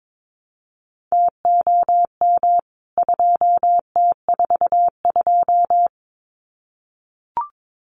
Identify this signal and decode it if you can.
Morse code, heard as a keyed tone, about 22 wpm, 700 Hz. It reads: TOM 2T42
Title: CWT Exchange 22wpm